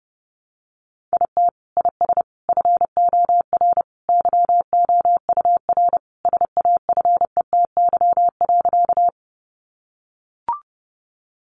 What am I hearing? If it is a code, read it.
Morse code, heard as a keyed tone, about 30 wpm, 700 Hz. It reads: IT IS FOR YOUR SAFETY.
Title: Sentences from Top 700 Words 30wpm